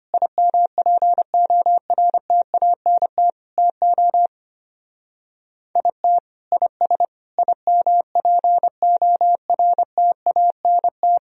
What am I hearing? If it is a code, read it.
Morse code, heard as a keyed tone, about 32 wpm, 700 Hz. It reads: IMPORTANT TO IT IS IMPORTANT